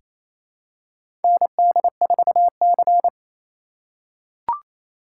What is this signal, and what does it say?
Morse code, heard as a keyed tone, about 28 wpm, 700 Hz. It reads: ND4C
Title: USA Call Signs - 4 Characters - 2x Character Spacing - 28wpm